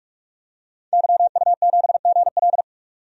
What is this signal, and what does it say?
Morse code, heard as a keyed tone, about 41 wpm, 700 Hz. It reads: YU7GL
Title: DXCC-100 Call Signs - 5 Characters - 45wpm